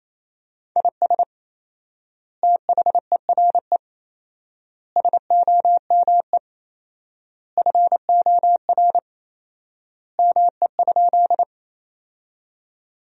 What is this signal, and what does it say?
Morse code, heard as a keyed tone, about 28 wpm, 700 Hz. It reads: IS THERE SOME FOR ME?
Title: 4x Word-Spacing - Sentences from Top 200 Words - 28wpm